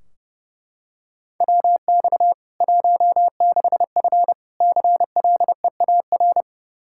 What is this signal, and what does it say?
Morse code, heard as a keyed tone, about 30 wpm, 700 Hz. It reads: WX 16F CLEAR
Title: K1USN SST QSO Practice 30wpm